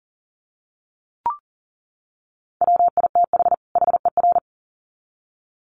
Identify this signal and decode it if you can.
Morse code, heard as a keyed tone, about 40 wpm, 700 Hz. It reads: WITH HER
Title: Sets of 2 Words 40wpm